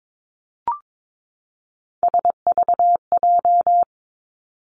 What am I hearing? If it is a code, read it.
Morse code, heard as a keyed tone, about 22 wpm, 700 Hz. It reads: SVJ
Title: Three Challenging Letter-Number Combinations 22wpm